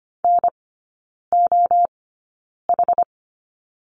Morse code, heard as a keyed tone, about 25 wpm, spaced as for 9 wpm, 700 Hz. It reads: NOH
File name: Lesson 14e - Number 2 - Three Characters 25wpm